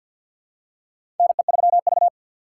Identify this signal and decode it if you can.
Morse code, heard as a keyed tone, about 53 wpm, 700 Hz. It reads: NE3V
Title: USA Call Signs - 4 Characters - 6x Character Spacing - 50wpm